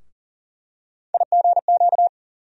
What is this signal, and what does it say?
Morse code, heard as a keyed tone, about 40 wpm, 700 Hz. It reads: IGQ
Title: USA Call Sign Postfixes - 3 Letters 40wpm